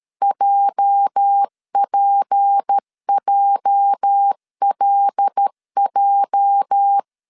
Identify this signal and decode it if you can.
Morse code, heard as a keyed tone, about 13 wpm, 800 Hz. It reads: JPJLJ
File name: LEZIONI / 8 YJ45 / 8 YJ 12wpm 2235